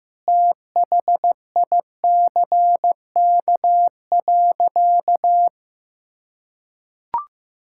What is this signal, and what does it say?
Morse code, heard as a keyed tone, about 15 wpm, 700 Hz. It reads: THICK.
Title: Sentences from Top 1600 Words 15wpm